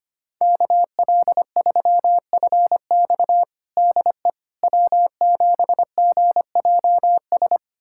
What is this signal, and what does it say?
Morse code, heard as a keyed tone, about 25 wpm, 700 Hz. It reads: KL3FX DE W7GJS
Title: USA Callsign DE Callsign - 5 Characters - 25wpm